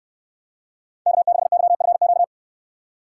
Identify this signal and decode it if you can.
Morse code, heard as a keyed tone, about 63 wpm, 700 Hz. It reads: K6YFQ